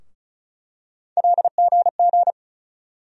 Morse code, heard as a keyed tone, about 36 wpm, 700 Hz. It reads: LGG
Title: USA Call Sign Postfixes - 3 Letters 35wpm